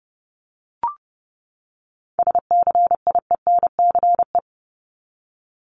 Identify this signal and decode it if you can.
Morse code, heard as a keyed tone, about 30 wpm, 700 Hz. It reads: SCIENCE
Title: Top 500 Words 30wpm